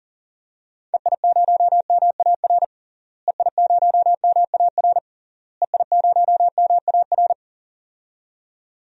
Morse code, heard as a keyed tone, about 40 wpm, 700 Hz. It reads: EI0MAR EI0MAR EI0MAR
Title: DXCC-100 Call Signs - 6 Characters - Repeated 3x - 40wpm